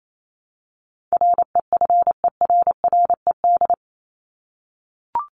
Morse code, heard as a keyed tone, about 28 wpm, 700 Hz. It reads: REFERRED
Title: ED Suffix - Top 100 Words with Base Words 28wpm